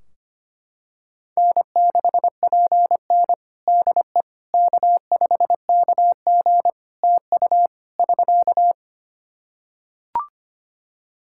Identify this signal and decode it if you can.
Morse code, heard as a keyed tone, about 25 wpm, 700 Hz. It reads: N6PN DE K5KG TU <SK>
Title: Practice QSO - Exchange 4 - Ending - 25wpm